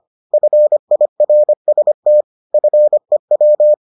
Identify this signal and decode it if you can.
Morse code, heard as a keyed tone, about 25 wpm, 600 Hz. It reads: FIRST FEW